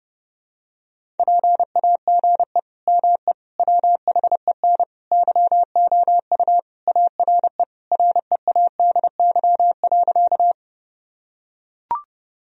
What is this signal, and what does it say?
Morse code, heard as a keyed tone, about 30 wpm, 700 Hz. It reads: PAGE ME WHEN YOU ARE READY.
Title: Sentences from Top 1000 Words - Review Part 1 - 30wpm